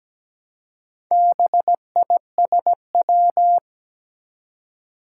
Morse code, heard as a keyed tone, about 17 wpm, 700 Hz. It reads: BISW
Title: Four Challenging Letter-Number Combinations 17wpm